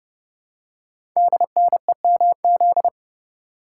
Morse code, heard as a keyed tone, about 30 wpm, 700 Hz. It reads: DNEMZ
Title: Code Groups - 5 Letters - 30wpm